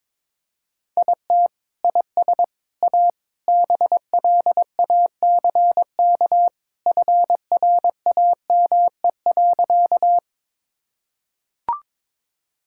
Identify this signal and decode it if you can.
Morse code, heard as a keyed tone, about 21 wpm, 700 Hz. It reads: IT IS A BLACK FRAME.
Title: Sentences from Top 1000 Words - Review Part 6 - 22wpm